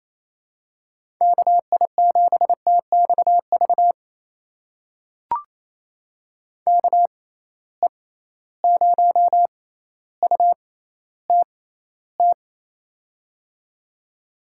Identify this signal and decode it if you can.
Morse code, heard as a keyed tone, about 28 wpm, spaced as for 14 wpm, 700 Hz. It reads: KI7TXV KE0UTT
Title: USA Call Signs - 6 Characters - 6x Character Spacing - 28wpm